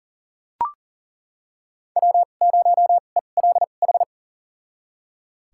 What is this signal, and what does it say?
Morse code, heard as a keyed tone, about 40 wpm, spaced as for 29 wpm, 700 Hz. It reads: W0ELH
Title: USA Call Signs - 5 Characters - 2x Character Spacing - 40wpm